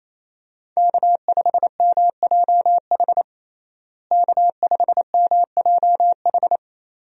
Morse code, heard as a keyed tone, about 28 wpm, 700 Hz. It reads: K5MJH K5MJH
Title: USA Call Signs - 5 Characters - Repeated 2x - 28wpm